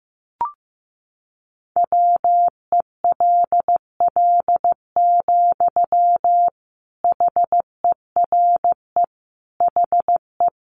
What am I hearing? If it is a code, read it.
Morse code, heard as a keyed tone, about 14 wpm, 700 Hz. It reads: WELL, HERE HE
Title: Sentences from Top 200 Words 15wpm